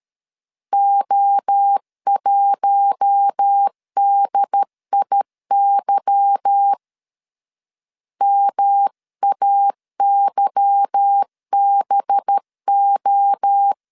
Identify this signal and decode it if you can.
Morse code, heard as a keyed tone, about 13 wpm, 800 Hz. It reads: O1DIY MAYBO
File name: LEZIONI / 8 YJ45 / 8 Y 12wpm 2228